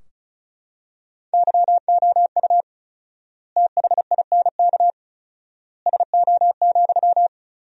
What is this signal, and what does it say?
Morse code, heard as a keyed tone, about 36 wpm, 700 Hz. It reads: YOU THINK SO,
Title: 4x Word-Spacing - Sentences from Top 200 Words - 35wpm